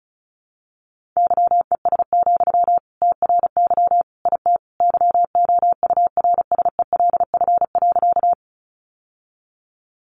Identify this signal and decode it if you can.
Morse code, heard as a keyed tone, about 34 wpm, 700 Hz. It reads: YES, TRY IT YOURSELF.
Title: Sentences from Top 600 Words 35wpm